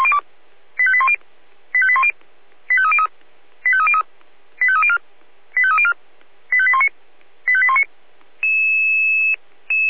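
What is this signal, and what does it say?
Five-tone selective call (ZVEI), heard as quick runs of five tones, tones of about 70 ms. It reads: 85810 85810 85303 85303 85304 85304 85819 85819
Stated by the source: ZVEI five-tone call 85101 received by SDR.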